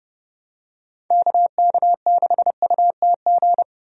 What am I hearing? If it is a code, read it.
Morse code, heard as a keyed tone, about 28 wpm, 700 Hz. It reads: KK6UTG
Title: USA Call Signs - 6 Characters - Repeated 3x - 30wpm